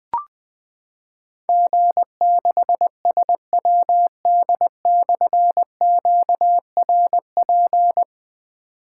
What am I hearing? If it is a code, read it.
Morse code, heard as a keyed tone, about 20 wpm, 700 Hz. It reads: G6SWD/QRP
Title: Difficult Call Signs 20wpm